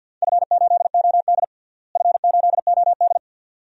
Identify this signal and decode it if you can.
Morse code, heard as a keyed tone, about 52 wpm, 700 Hz. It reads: F8OD F8OD
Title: DXCC-100 Call Signs - 4 Characters - Repeated 2x - 50wpm